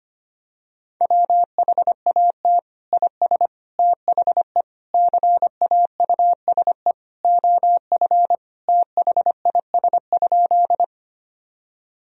Morse code, heard as a keyed tone, about 25 wpm, 700 Hz. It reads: WHAT IS THE CAUSE OF THIS?
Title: Sentences from Top 1000 Words - Review Part 1 - 25wpm